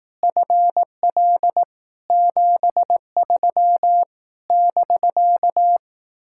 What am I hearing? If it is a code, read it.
Morse code, heard as a keyed tone, about 18 wpm, 700 Hz. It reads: FL 73 <BK>
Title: Speed-Racing Common QSO Phrases Encore - 15wpm